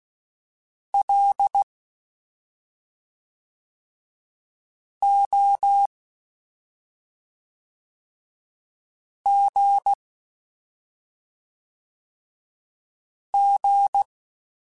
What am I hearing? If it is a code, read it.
Morse code, heard as a keyed tone, about 16 wpm, 800 Hz. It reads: L O G G
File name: Special INGs / Special INGs-mp3s / SlowToFaster ing @5